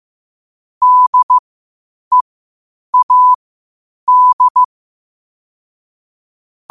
Morse code, heard as a keyed tone, about 15 wpm, spaced as for 8 wpm, 1000 Hz. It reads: DEAD
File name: qotdfiles / 8wpm